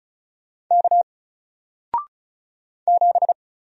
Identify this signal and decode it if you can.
Morse code, heard as a keyed tone, about 36 wpm, 700 Hz. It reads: K 7
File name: Single-Letter-Number-35wpm